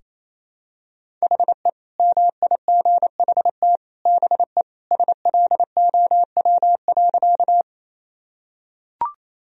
Morse code, heard as a keyed tone, about 28 wpm, 700 Hz. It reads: HE MIGHT BE SLOW.
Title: Sentences from Top 1000 Words - Review Part 2 - 28wpm